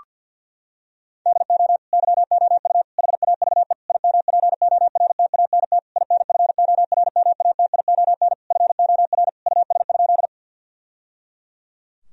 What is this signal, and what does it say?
Morse code, heard as a keyed tone, about 50 wpm, 700 Hz. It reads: DO YOU HAVE IMPORTANT INFORMATION FOR US?
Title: Sentences from Top 1000 Words - Review Part 3 - 50wpm